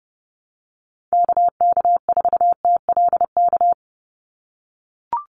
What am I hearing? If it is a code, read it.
Morse code, heard as a keyed tone, about 30 wpm, 700 Hz. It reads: KK4TLK
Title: USA Call Signs 30wpm